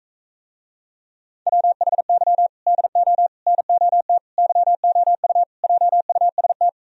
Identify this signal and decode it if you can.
Morse code, heard as a keyed tone, about 45 wpm, 700 Hz. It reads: WHY DO NOT YOU JUST